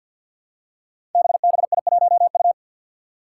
Morse code, heard as a keyed tone, about 48 wpm, 700 Hz. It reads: BBI1U
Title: Five Challenging Letter-Number Combinations 50wpm